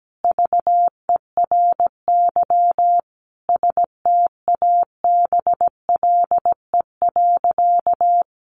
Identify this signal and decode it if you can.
Morse code, heard as a keyed tone, about 17 wpm, 700 Hz. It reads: VERY STABLE.
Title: Sentences from Top 1500 Words 17wpm